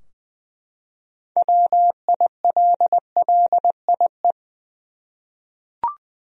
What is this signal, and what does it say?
Morse code, heard as a keyed tone, about 21 wpm, 700 Hz. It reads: WILLIE
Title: USA Common Names - 20wpm